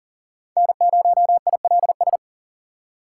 Morse code, heard as a keyed tone, about 40 wpm, 700 Hz. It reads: N0ILS
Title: USA Call Signs - 4x Character Spacing - 40wpm